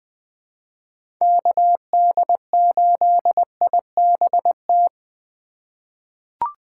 Morse code, heard as a keyed tone, about 20 wpm, 700 Hz. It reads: KD8IBT
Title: USA Call Signs - 6 Characters - 2x Character Spacing - 20wpm